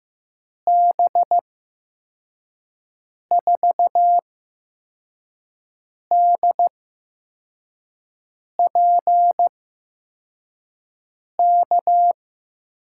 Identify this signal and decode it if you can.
Morse code, heard as a keyed tone, about 15 wpm, spaced as for 4 wpm, 700 Hz. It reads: B4DPK
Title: USA Call Signs - 6 Characters - 8x Character Spacing - 15wpm